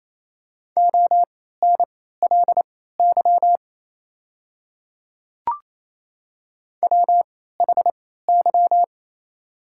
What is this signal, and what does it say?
Morse code, heard as a keyed tone, about 27 wpm, spaced as for 16 wpm, 690 Hz. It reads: ONLY WHY
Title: 3x Character Spacing - Top 200 Words - 28wpm